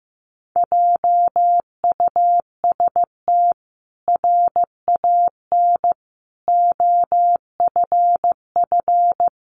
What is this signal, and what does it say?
Morse code, heard as a keyed tone, about 15 wpm, 700 Hz. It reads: JUST RAN OFF
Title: A Sound Of Thunder 15wpm